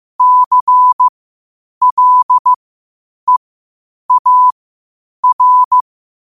Morse code, heard as a keyed tone, about 15 wpm, spaced as for 8 wpm, 1000 Hz. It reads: CLEAR
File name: qotdfiles / 8wpm